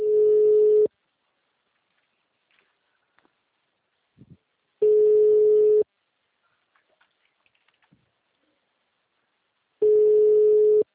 Ringback tone (Europe, 425 Hz).
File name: out- -2002- - - .71